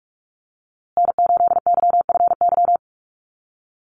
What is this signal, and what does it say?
Morse code, heard as a keyed tone, about 44 wpm, 700 Hz. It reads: N8YFY
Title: USA Call Signs Repeated 2x 45wpm